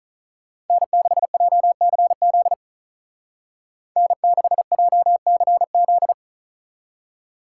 Morse code, heard as a keyed tone, about 47 wpm, 700 Hz. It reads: N6JCZ N6JCZ